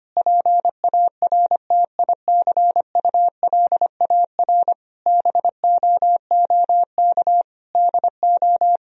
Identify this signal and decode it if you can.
Morse code, heard as a keyed tone, about 25 wpm, 700 Hz. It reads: PARTICULAR BOOK DO
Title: Sentences from Top 1000 Words - Review Part 4 - 25wpm